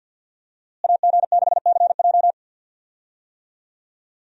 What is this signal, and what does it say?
Morse code, heard as a keyed tone, about 49 wpm, 700 Hz. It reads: AG6CJ
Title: USA Call Signs - 5 Characters - Repeated 3x - 50wpm